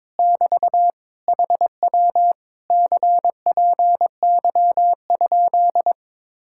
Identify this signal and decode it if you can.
Morse code, heard as a keyed tone, about 22 wpm, 700 Hz. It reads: <BT> HW CPY?
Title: Speed-Racing Common QSO Phrases - 22wpm